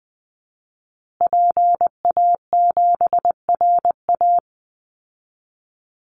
Morse code, heard as a keyed tone, about 19 wpm, 700 Hz. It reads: PA7RA
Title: Call Signs Repeated 3x - 20wpm